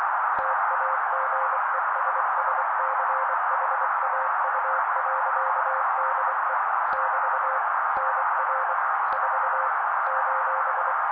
Morse code, heard as a keyed tone, about 24 wpm, 530 Hz. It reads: TAGESSCHAU.DE <BT> NR 4 :
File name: daily / 23wpm noise